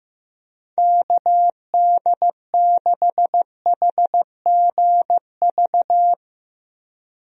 Morse code, heard as a keyed tone, about 15 wpm, 700 Hz. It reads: KD6HGV
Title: USA Call Signs - 6 Characters - 15wpm